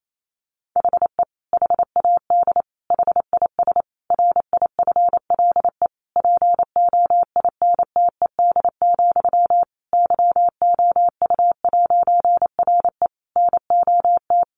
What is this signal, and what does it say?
Morse code, heard as a keyed tone, about 28 wpm, 700 Hz. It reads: HE HAD HIS RIFLE POINTED, YOU'RE NOT
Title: A Sound Of Thunder 28wpm